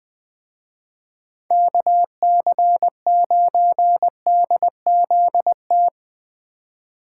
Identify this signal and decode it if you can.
Morse code, heard as a keyed tone, about 20 wpm, 700 Hz. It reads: KC9DZT
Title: Speed Racing - USA 2x3 Call Signs 17wpm